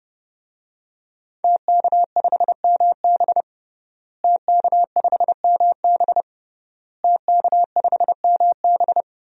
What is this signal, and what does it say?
Morse code, heard as a keyed tone, about 30 wpm, 700 Hz. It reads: TK5MB TK5MB TK5MB
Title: DXCC-100 Call Signs - 5 Characters - Repeated 3x - 30wpm